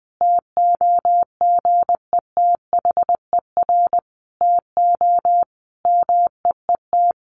Morse code, heard as a keyed tone, about 20 wpm, 700 Hz. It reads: TOGETHER TO MEET
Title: Sentences from Top 1100 Words 20wpm